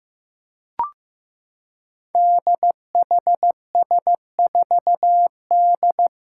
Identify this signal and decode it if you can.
Morse code, heard as a keyed tone, about 15 wpm, 700 Hz. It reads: DHS4D
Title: Five Challenging Letter-Number Combinations 15wpm